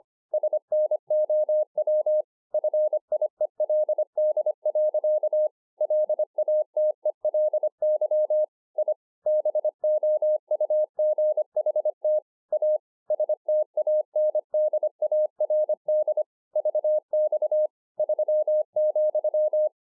Morse code, heard as a keyed tone, about 25 wpm, 610 Hz. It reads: SNOW FIELD. LATELY I BOUGHT A STANDARD VX 3,